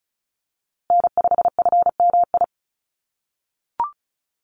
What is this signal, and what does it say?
Morse code, heard as a keyed tone, about 34 wpm, 700 Hz. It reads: N5FMI
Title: USA Call Signs Repeated 3x 35wpm